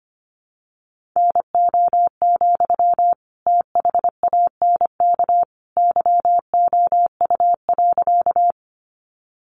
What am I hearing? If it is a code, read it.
Morse code, heard as a keyed tone, about 25 wpm, 700 Hz. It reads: NO, THANK YOU.
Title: Sentences from Top 600 Words 25wpm